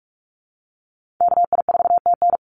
Morse code, heard as a keyed tone, about 44 wpm, 700 Hz. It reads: KI4TN
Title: NA SOTA Chaser Callsigns 45wpm